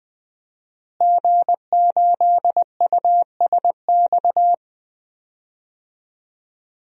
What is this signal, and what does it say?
Morse code, heard as a keyed tone, about 20 wpm, 700 Hz. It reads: G8USX
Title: DXCC-100 Call Signs - 5 Characters - Repeated 3x - 20wpm